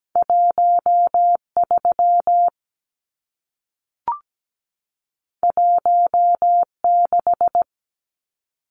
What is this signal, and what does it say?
Morse code, heard as a keyed tone, about 17 wpm, 700 Hz. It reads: 13 16
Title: Numbers - 2-Digit 17wpm